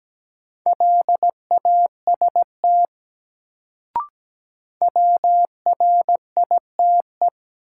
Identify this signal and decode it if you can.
Morse code, heard as a keyed tone, about 17 wpm, 700 Hz. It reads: LAST WRITE
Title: Sentences from Top 300 Words - New Word Drill - 17wpm